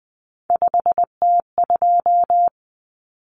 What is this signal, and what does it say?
Morse code, heard as a keyed tone, about 20 wpm, 700 Hz. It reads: 5T2
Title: Common Cut Numbers - 3-Digit 20wpm